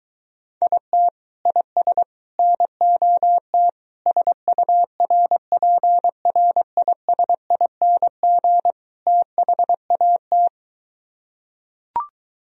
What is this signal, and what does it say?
Morse code, heard as a keyed tone, about 24 wpm, 700 Hz. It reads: IT IS NOT SURPRISING THAT